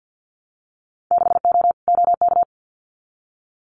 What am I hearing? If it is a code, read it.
Morse code, heard as a keyed tone, about 50 wpm, 700 Hz. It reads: 6O OK
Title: Winter Field Day Exchanges Encore 50wpm